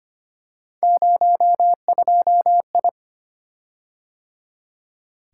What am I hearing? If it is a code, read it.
Morse code, heard as a keyed tone, about 25 wpm, 700 Hz. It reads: 02I
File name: Three Letter-Number Combinations - 25wpm